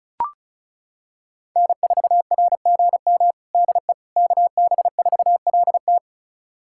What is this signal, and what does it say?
Morse code, heard as a keyed tone, about 36 wpm, 700 Hz. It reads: N4RGM DE KB4LT
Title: USA Callsign DE Callsign - 5 Characters - 35wpm